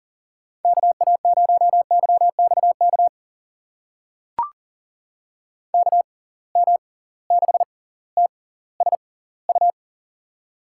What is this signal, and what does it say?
Morse code, heard as a keyed tone, about 40 wpm, spaced as for 20 wpm, 700 Hz. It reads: KA0YXK KM6TSU
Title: USA Call Signs - 6 Characters - 6x Character Spacing - 40wpm